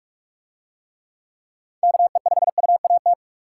Morse code, heard as a keyed tone, about 44 wpm, 700 Hz. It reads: KE5UAT